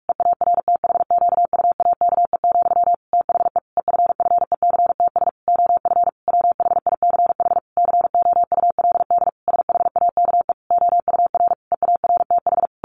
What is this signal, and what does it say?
Morse code, heard as a keyed tone, about 44 wpm, 700 Hz. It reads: EARTHQUAKE, THE EFFECTS OF WHICH COULD SHAKE OUR EARTH
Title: A Sound Of Thunder 45wpm